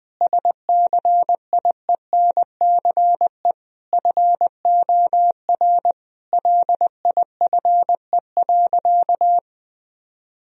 Morse code, heard as a keyed tone, about 20 wpm, 700 Hz. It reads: SCIENCE FOR LIFE.
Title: Sentences from Top 1000 Words - Review Part 2 - 20wpm